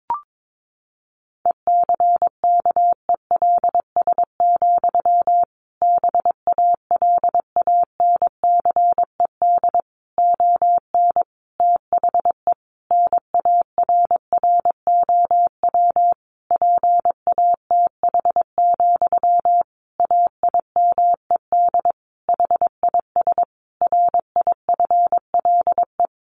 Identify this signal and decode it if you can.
Morse code, heard as a keyed tone, about 22 wpm, 700 Hz. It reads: ECKELS, BALANCED ON THE NARROW PATH, AIMED HIS RIFLE
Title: A Sound Of Thunder 22wpm